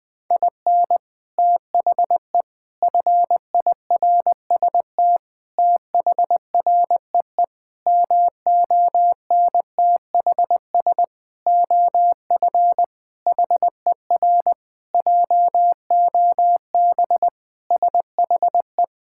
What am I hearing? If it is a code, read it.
Morse code, heard as a keyed tone, about 20 wpm, 700 Hz. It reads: IN THE FIRST THREE MONTHS OF HER JOB SHE
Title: Sentences with 6-Word Phrases - Spoken Beginning - 20wpm